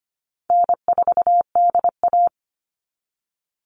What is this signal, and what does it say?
Morse code, heard as a keyed tone, about 25 wpm, 700 Hz. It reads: N4DA
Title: NA SOTA Chaser Callsigns 25wpm